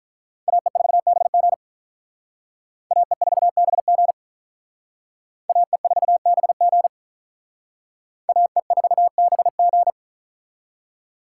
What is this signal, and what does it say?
Morse code, heard as a keyed tone, about 44 wpm, 700 Hz. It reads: AE4BG AE4BG AE4BG AE4BG